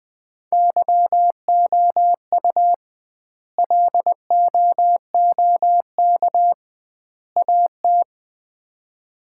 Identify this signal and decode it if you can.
Morse code, heard as a keyed tone, about 21 wpm, 700 Hz. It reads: YOU LOOK AT
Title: 2x Word-Spacing - Sets of 3 Words - 20wpm